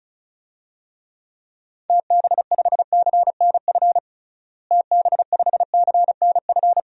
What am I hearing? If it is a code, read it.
Morse code, heard as a keyed tone, about 36 wpm, 700 Hz. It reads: TB5CNF TB5CNF
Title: DXCC-100 Call Signs - 6 Characters - Repeated 2x - 35wpm